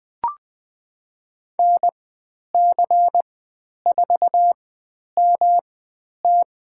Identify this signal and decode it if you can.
Morse code, heard as a keyed tone, about 20 wpm, spaced as for 10 wpm, 700 Hz. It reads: NC4MT
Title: 109 Lesson 18g - Letter m - Call Signs 20wpm